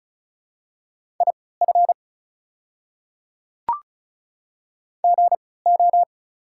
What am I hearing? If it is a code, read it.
Morse code, heard as a keyed tone, about 36 wpm, spaced as for 20 wpm, 690 Hz. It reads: IF GO
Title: 3x Character Spacing - Top 100 Words - 35wpm